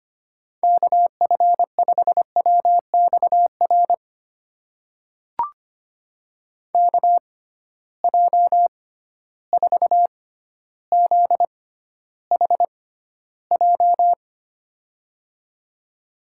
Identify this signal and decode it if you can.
Morse code, heard as a keyed tone, about 25 wpm, spaced as for 13 wpm, 700 Hz. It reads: KF5WXR KJ4ZHJ
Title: USA Call Signs - 6 Characters - 6x Character Spacing - 25wpm